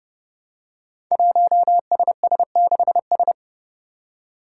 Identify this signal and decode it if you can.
Morse code, heard as a keyed tone, about 28 wpm, 700 Hz. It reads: 1SS6S
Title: Five Challenging Letter-Number Combinations 30wpm